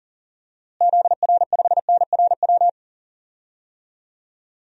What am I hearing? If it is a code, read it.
Morse code, heard as a keyed tone, about 40 wpm, 700 Hz. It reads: ZR5NRW
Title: DXCC-100 Call Signs - 6 Characters - Repeated 3x - 40wpm